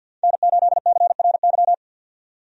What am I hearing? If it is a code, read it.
Morse code, heard as a keyed tone, about 50 wpm, 700 Hz. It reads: N8CRY
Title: USA Call Signs - 5 Characters - 50wpm